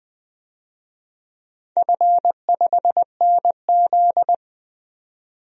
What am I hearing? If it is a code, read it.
Morse code, heard as a keyed tone, about 20 wpm, 700 Hz. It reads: F5NZ
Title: DXCC-100 Call Signs - 4 Characters - 20wpm